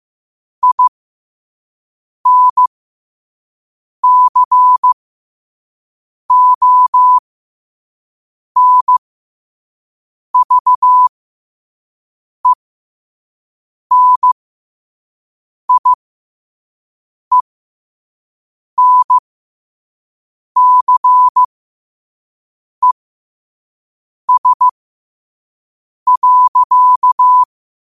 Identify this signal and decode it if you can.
Morse code, heard as a keyed tone, about 15 wpm, spaced as for 5 wpm, 1000 Hz. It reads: INCONVENIENCES.
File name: qotdfiles / 5wpm